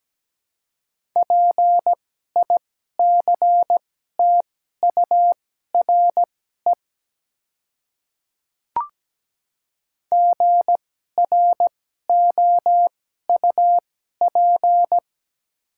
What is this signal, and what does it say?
Morse code, heard as a keyed tone, about 17 wpm, spaced as for 12 wpm, 700 Hz. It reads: PICTURE GROUP
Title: 2x Character Spacing - Top 300 Words - 17wpm